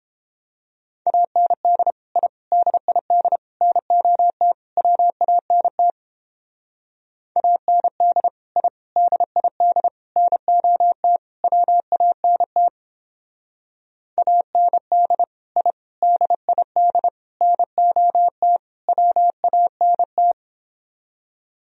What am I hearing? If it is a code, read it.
Morse code, heard as a keyed tone, about 29 wpm, 700 Hz. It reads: AND I DID NOT WANT AND I DID NOT WANT AND I DID NOT WANT